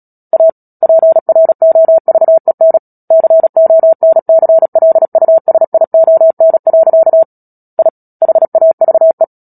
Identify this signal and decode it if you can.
Morse code, heard as a keyed tone, about 37 wpm, 650 Hz. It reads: A PROVEN CONCLUSION. I HAVE
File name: Book Two / 10-Chapter 10 - Epilogue - 36 WPM